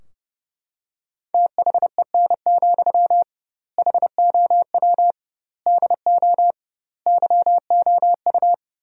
Morse code, heard as a keyed tone, about 30 wpm, 700 Hz. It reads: THEN, HOW DO YOU
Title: 2x Word-Spacing - Sentences from Top 100 Words - 30wpm